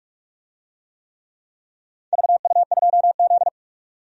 Morse code, heard as a keyed tone, about 44 wpm, 690 Hz. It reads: VU2Z
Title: DXCC-100 Call Signs - 4 Characters - 45wpm